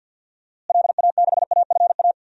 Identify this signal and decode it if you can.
Morse code, heard as a keyed tone, about 50 wpm, 700 Hz. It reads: LA6AFA
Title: DXCC-100 Call Signs - 6 Characters - Repeated 2x - 50wpm